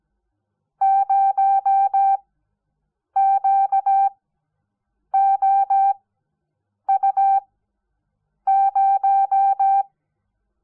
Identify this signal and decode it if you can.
Morse code, heard as a keyed tone, about 17 wpm, spaced as for 7 wpm, 780 Hz. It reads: 0QOU0
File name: Onlinelehrgang / morsen / DJ4UF Morsen05